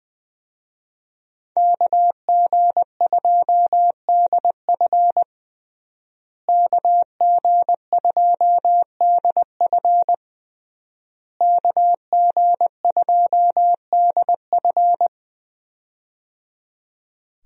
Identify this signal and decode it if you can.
Morse code, heard as a keyed tone, about 20 wpm, 700 Hz. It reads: KG2DF KG2DF KG2DF
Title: USA Call Signs - 5 Characters - Repeated 3x - 20wpm